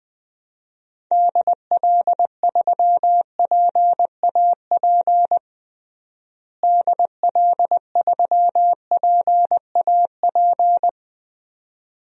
Morse code, heard as a keyed tone, about 21 wpm, 700 Hz. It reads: DL3PAP DL3PAP
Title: DXCC-100 Call Signs - 6 Characters - Repeated 2x - 20wpm